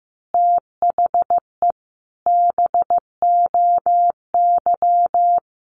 Morse code, heard as a keyed tone, about 15 wpm, 700 Hz. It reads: THE BOY
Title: Easy Phrases 15wpm